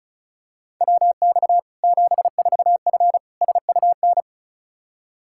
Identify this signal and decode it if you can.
Morse code, heard as a keyed tone, about 36 wpm, 700 Hz. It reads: WX 74F SUN
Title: Practice QSO - Exchange 2 - Info - 35wpm